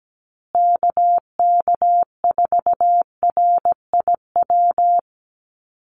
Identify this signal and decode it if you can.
Morse code, heard as a keyed tone, about 17 wpm, 700 Hz. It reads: KK4RIW
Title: USA Call Signs Repeated 3x 17wpm